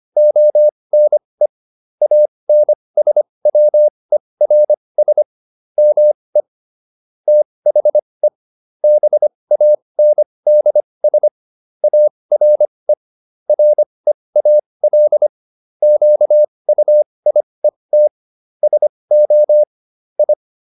Morse code, heard as a keyed tone, about 25 wpm, spaced as for 20 wpm, 600 Hz. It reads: ONE ANSWERS ME THE BANDS ARE REAL QUIET SO I